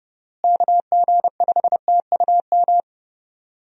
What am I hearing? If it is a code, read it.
Morse code, heard as a keyed tone, about 30 wpm, 700 Hz. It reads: KG5TUM
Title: USA Call Signs - 6 Characters - Repeated 3x - 30wpm